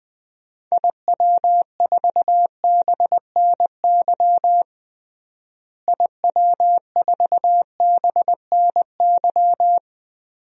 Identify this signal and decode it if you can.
Morse code, heard as a keyed tone, about 20 wpm, 700 Hz. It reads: IW4BNY IW4BNY
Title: DXCC-100 Call Signs - 6 Characters - Repeated 3x - 20wpm